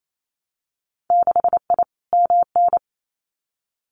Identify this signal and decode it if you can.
Morse code, heard as a keyed tone, about 28 wpm, 690 Hz. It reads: 6I MN
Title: Winter Field Day Exchanges Encore 28wpm